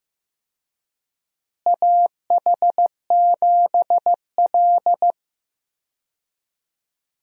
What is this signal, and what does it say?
Morse code, heard as a keyed tone, about 15 wpm, 700 Hz. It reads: AH7L
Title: USA Call Signs - 4 Characters - Repeated 3x - 15wpm